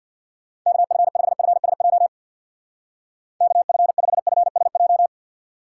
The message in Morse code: KF5FSJ KF5FSJ

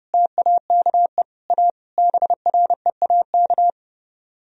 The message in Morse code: TAKE A BREAK

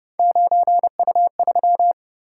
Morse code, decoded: 9U3

